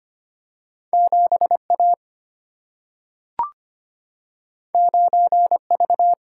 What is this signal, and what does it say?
Morse code, heard as a keyed tone, about 25 wpm, 700 Hz. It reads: 7A 9V
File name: Two Letter-Number Combinations - 25wpm